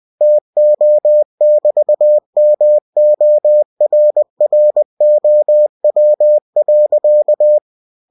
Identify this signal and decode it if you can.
Morse code, heard as a keyed tone, about 20 wpm, 600 Hz. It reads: TO<BT>MORROW.
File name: cw-books / winnie-the-pooh / winnie-20wpm-page-0102